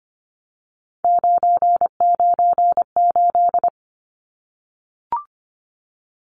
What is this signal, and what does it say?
Morse code, heard as a keyed tone, about 25 wpm, 700 Hz. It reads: 998
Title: Numbers - 3-Digit 25wpm